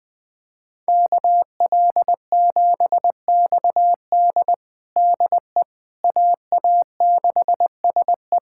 Morse code, heard as a keyed tone, about 20 wpm, 700 Hz. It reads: KL7XD DE AA6SE
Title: USA Callsign DE Callsign - 5 Characters - 20wpm